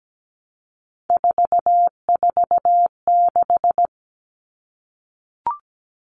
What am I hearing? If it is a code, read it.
Morse code, heard as a keyed tone, about 17 wpm, 700 Hz. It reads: 446
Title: Numbers - 3-Digit 17wpm